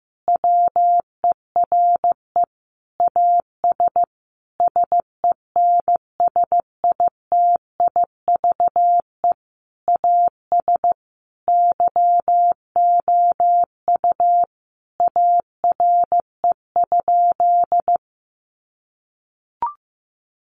Morse code, heard as a keyed tone, about 15 wpm, 700 Hz. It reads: WERE AS SENSITIVE AS YOU ARE?
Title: Sentences from Top 1400 Words 15wpm